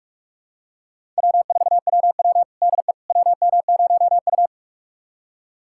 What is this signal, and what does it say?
Morse code, heard as a keyed tone, about 41 wpm, 700 Hz. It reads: W4WW DE WM0U
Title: USA Callsign DE Callsign - 4 Characters - 45wpm